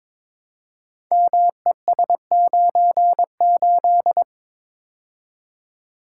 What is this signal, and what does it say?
Morse code, heard as a keyed tone, about 22 wpm, 700 Hz. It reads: MES98
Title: Code Groups - 5 Letter:Numbers - 22wpm